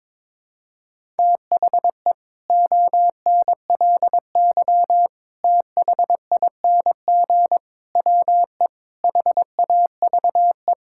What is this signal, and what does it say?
Morse code, heard as a keyed tone, about 22 wpm, 700 Hz. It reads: THE ONLY THING WE HAVE
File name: Sets of 9 Words - 22wpm